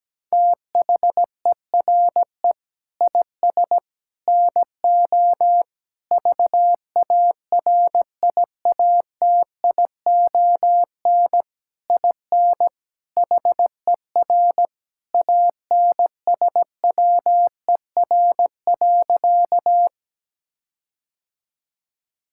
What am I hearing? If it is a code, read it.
Morse code, heard as a keyed tone, about 17 wpm, 700 Hz. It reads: THERE IS NO VARIATION IN HER ANSWER.
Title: Sentences from Top 1600 Words 17wpm